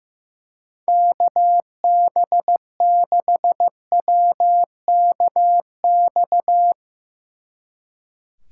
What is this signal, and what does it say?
Morse code, heard as a keyed tone, about 16 wpm, 700 Hz. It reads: KB6WKX
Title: Speed Racing - USA 2x3 Call Signs 15wpm